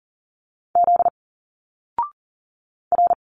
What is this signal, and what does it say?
Morse code, heard as a keyed tone, about 39 wpm, 700 Hz. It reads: Z R
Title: Single Letters 40wpm